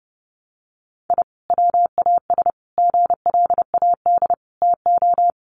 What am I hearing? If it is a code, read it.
Morse code, heard as a keyed tone, about 30 wpm, 700 Hz. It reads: I WAS GLAD TO
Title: Sentences from Top 1100 Words 30wpm